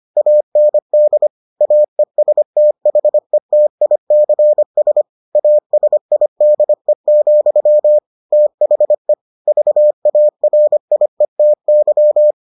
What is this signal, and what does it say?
Morse code, heard as a keyed tone, about 25 wpm, 600 Hz. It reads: AND AESTHETICS ASIDE, THE VARIETY